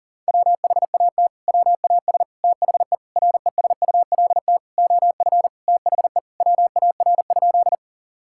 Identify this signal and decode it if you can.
Morse code, heard as a keyed tone, about 40 wpm, 700 Hz. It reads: WHAT WAS THE RESULT OF THE WAR?
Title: Sentences from Top 1000 Words - Review Part 5 - 40wpm